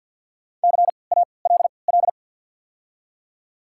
Morse code, heard as a keyed tone, about 54 wpm, spaced as for 29 wpm, 700 Hz. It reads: CALL
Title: 3x Character Spacing - Top 200 Words - 50wpm